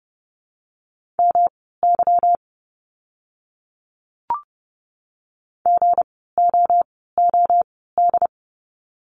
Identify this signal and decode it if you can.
Morse code, heard as a keyed tone, about 30 wpm, spaced as for 17 wpm, 700 Hz. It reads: MY GOOD
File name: course / 30:10wpm / 230 Lesson 37f - Letter x - Words 30wpm